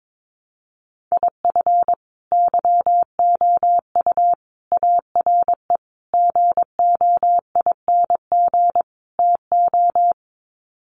Morse code, heard as a keyed tone, about 22 wpm, 700 Hz. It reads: IF YOU ARE GOING TO
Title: Sets of 5 Words Encore2 22wpm